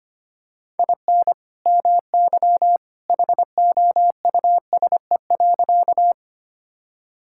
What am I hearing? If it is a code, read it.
Morse code, heard as a keyed tone, about 25 wpm, 700 Hz. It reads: IN MY HOUSE.
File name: practice / 25wpm / Sentences with 2-Word Phrases - 25wpm